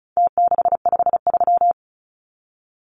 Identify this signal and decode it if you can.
Morse code, heard as a keyed tone, about 34 wpm, 700 Hz. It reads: T653
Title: Numbers - Contest Style - 4-Digit 35wpm